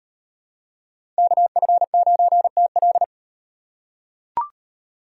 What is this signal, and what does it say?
Morse code, heard as a keyed tone, about 38 wpm, 700 Hz. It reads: KF9TL